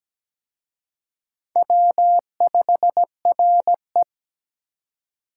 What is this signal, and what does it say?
Morse code, heard as a keyed tone, about 20 wpm, 700 Hz. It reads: W5RE
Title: Speed Racing - USA 1x2 Call Signs 17wpm